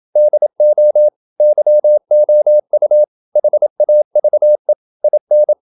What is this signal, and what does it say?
Morse code, heard as a keyed tone, about 27 wpm, 600 Hz. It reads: DO YOU HAVE IN
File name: Never 0027wpm